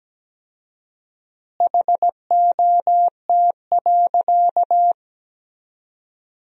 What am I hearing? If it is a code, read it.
Morse code, heard as a keyed tone, about 17 wpm, 700 Hz. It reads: HOT.
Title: 4x Word-Spacing - Sentences from Top 500 Words - 17wpm